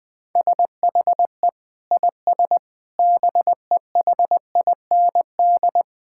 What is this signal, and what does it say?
Morse code, heard as a keyed tone, about 20 wpm, 700 Hz. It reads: SHE IS BEHIND
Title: Sentences from Top 1000 Words - Review Part 6 - 20wpm